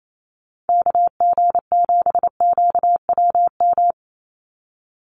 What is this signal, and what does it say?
Morse code, heard as a keyed tone, about 28 wpm, 700 Hz. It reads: KG7QWM
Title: USA Call Signs 28wpm